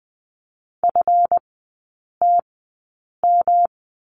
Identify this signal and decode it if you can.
Morse code, heard as a keyed tone, about 20 wpm, spaced as for 8 wpm, 700 Hz. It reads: FTM
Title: 152 Lesson 25e - Letter p - Three Characters 20wpm